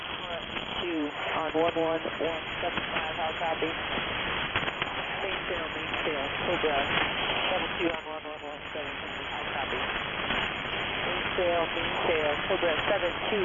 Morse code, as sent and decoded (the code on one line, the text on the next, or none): none